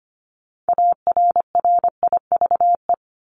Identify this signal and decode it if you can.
Morse code, heard as a keyed tone, about 25 wpm, 700 Hz. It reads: ARRIVE
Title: ED Suffix - Top 100 Words with Base Words 25wpm